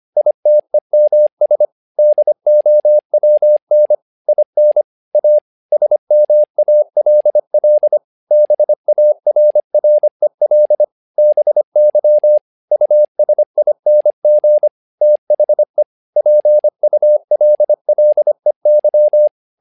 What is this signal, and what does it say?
Morse code, heard as a keyed tone, about 25 wpm, 600 Hz. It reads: ITEMS DOWN IN A SMALL BARREL BY USING THE PULLEY
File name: learning center / tower trouble / tower trouble 25wpm